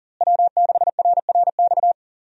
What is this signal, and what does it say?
Morse code, heard as a keyed tone, about 39 wpm, 700 Hz. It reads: W6RRX
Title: USA Call Signs - 5 Characters - Repeated 2x - 40wpm